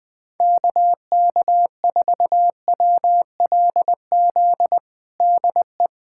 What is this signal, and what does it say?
Morse code, heard as a keyed tone, about 20 wpm, 700 Hz. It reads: KK4WLZ DE
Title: USA Callsign DE Callsign - 6 Characters - 20wpm